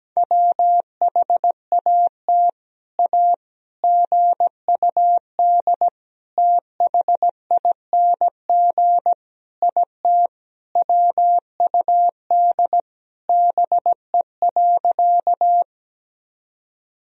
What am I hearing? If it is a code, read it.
Morse code, heard as a keyed tone, about 17 wpm, 700 Hz. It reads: WHAT A GUD THING IT WUD BE.
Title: Sentences from Top 200 Words and Abbreviations 17wpm